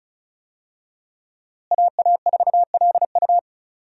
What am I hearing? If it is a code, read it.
Morse code, heard as a keyed tone, about 36 wpm, 700 Hz. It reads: AA4LU
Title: USA Call Signs - 5 Characters - Repeated 3x - 35wpm